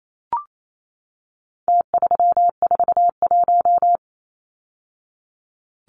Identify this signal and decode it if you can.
Morse code, heard as a keyed tone, about 28 wpm, 700 Hz. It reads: T341
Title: Numbers - Contest Style - 4-Digit 28wpm